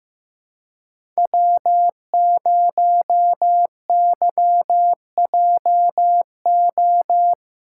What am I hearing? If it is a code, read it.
Morse code, heard as a keyed tone, about 15 wpm, 700 Hz. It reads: W0YJO
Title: USA Call Signs - 5 Characters - 6x Character Spacing - 15wpm